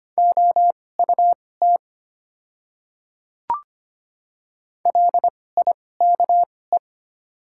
Morse code, heard as a keyed tone, about 25 wpm, spaced as for 18 wpm, 690 Hz. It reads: OUT LIKE
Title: 2x Character Spacing - Top 200 Words - 25wpm